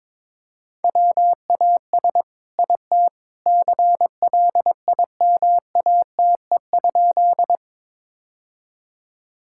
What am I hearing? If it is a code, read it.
Morse code, heard as a keyed tone, about 21 wpm, 700 Hz. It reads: WAS IT CLIMATE?
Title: Sentences from Top 1600 Words 22wpm